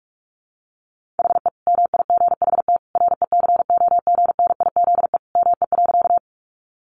Morse code, heard as a keyed tone, about 44 wpm, 700 Hz. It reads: HE MIGHT RECOGNIZE ME.
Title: Sentences from Top 1400 Words 45wpm